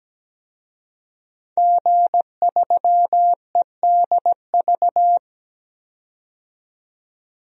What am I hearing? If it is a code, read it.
Morse code, heard as a keyed tone, about 17 wpm, 700 Hz. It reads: G3EDV
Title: DXCC-100 Call Signs - 5 Characters - 17wpm